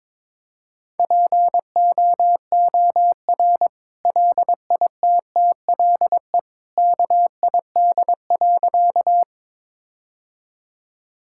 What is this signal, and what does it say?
Morse code, heard as a keyed tone, about 21 wpm, 700 Hz. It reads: POOR LITTLE KID.
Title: Sentences from Top 1600 Words 22wpm